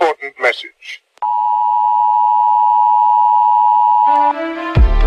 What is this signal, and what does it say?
Emergency Alert System (SAME) tones (the 853 and 960 Hz attention tone), with speech. It is heard from 1.2 to 4.4 s.